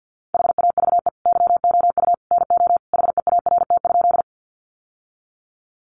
Morse code, heard as a keyed tone, about 50 wpm, 700 Hz. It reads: HAVE YOU NO HEART?
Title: Sentences from Top 1200 Words 50wpm